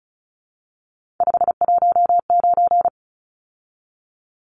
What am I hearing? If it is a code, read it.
Morse code, heard as a keyed tone, about 34 wpm, 700 Hz. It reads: 519
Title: Numbers - 3-Digit 35wpm